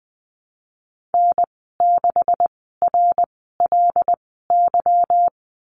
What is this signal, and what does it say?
Morse code, heard as a keyed tone, about 20 wpm, spaced as for 15 wpm, 700 Hz. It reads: N6RLY
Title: 200 Lesson 32g - Letter k - Call Signs 20wpm